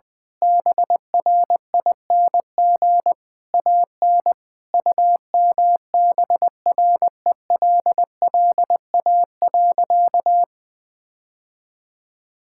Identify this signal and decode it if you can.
Morse code, heard as a keyed tone, about 20 wpm, 700 Hz. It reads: BRING AN UMBRELLA.
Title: Sentences with 3-Word Phrases - 20wpm